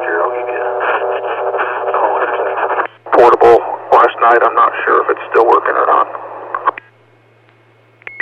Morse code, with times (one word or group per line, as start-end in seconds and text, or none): none